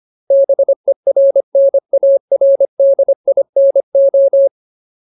Morse code, heard as a keyed tone, about 25 wpm, 550 Hz. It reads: BERNARDINO
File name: Ultimate Word List / 25WPM Ultimate